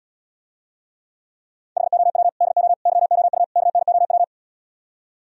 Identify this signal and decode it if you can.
Morse code, heard as a keyed tone, about 77 wpm, 700 Hz. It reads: HOW DO YOU KNOW